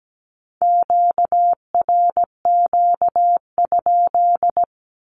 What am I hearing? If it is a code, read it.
Morse code, heard as a keyed tone, about 17 wpm, 700 Hz. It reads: QRQ?
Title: Q-Codes with Spoken Meaning - 17wpm